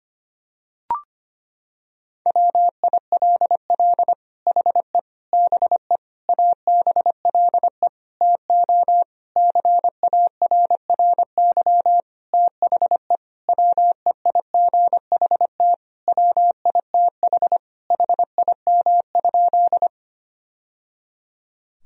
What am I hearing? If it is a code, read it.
Morse code, heard as a keyed tone, about 25 wpm, 700 Hz. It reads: WILL HE BE ABLE TO CARRY THE WEIGHT WITH HIM?
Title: Sentences from Top 1000 Words - Review Part 3 - 25wpm